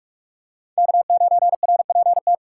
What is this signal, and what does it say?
Morse code, heard as a keyed tone, about 44 wpm, 700 Hz. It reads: K9RPT
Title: USA Call Signs - 2x Character Spacing - 45wpm